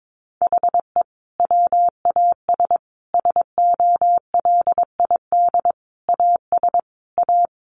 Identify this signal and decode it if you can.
Morse code, heard as a keyed tone, about 22 wpm, 700 Hz. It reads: HE WAS SOLID AS A
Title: Sentences from Top 1500 Words 22wpm